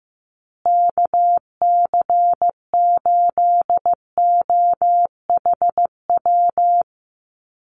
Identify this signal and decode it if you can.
Morse code, heard as a keyed tone, about 15 wpm, 700 Hz. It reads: KC8OHW
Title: USA Call Signs Repeated 3x 15wpm